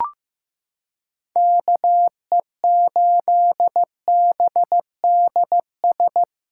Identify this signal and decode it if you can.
Morse code, heard as a keyed tone, about 15 wpm, 700 Hz. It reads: KE8BDS
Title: USA Callsign DE Callsign 15wpm